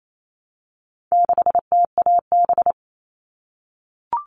Morse code, heard as a keyed tone, about 28 wpm, 700 Hz. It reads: 6TAB
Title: All Cut Numbers - 4-Digit 28wpm